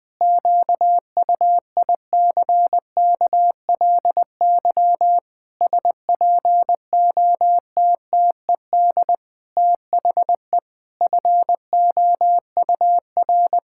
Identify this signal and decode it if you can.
Morse code, heard as a keyed tone, about 20 wpm, 700 Hz. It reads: QUICKLY SPOTTED THE FOUR
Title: Pangram Challenge v2 - 20wpm